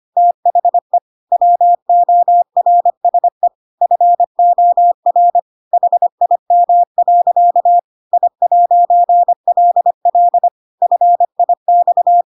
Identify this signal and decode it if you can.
Morse code, heard as a keyed tone, about 25 wpm, 700 Hz. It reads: THE WORSE FOR HIM. I'LL FIX
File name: Pinocchio / Pinocchio ch1 25 25 ik1hge